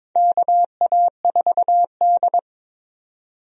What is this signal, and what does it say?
Morse code, heard as a keyed tone, about 21 wpm, 700 Hz. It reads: KA4D
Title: USA Call Signs - 4 Characters - 22wpm